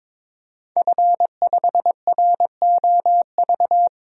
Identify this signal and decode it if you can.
Morse code, heard as a keyed tone, about 22 wpm, 700 Hz. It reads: F5ROV
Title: DXCC-100 Call Signs - 5 Characters - 22wpm